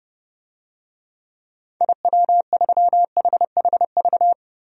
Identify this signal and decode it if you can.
Morse code, heard as a keyed tone, about 32 wpm, 700 Hz. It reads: IW3HHV
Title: DXCC-100 Call Signs - 6 Characters - 30wpm